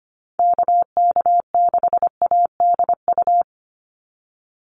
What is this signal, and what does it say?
Morse code, heard as a keyed tone, about 25 wpm, 700 Hz. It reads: KK6ADU
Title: USA Call Signs 25wpm